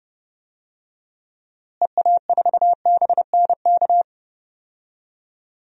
Morse code, heard as a keyed tone, about 30 wpm, 700 Hz. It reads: EA4BNK
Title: DXCC-100 Call Signs - 6 Characters - Repeated 2x - 30wpm